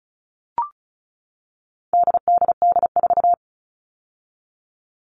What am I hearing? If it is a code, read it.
Morse code, heard as a keyed tone, about 34 wpm, 700 Hz. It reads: DDD4
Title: All Cut Numbers - 4-Digit 35wpm